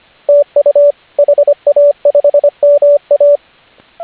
Morse code, heard as a keyed tone, about 38 wpm, spaced as for 32 wpm, 580 Hz, with radio noise. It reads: TU HA5MA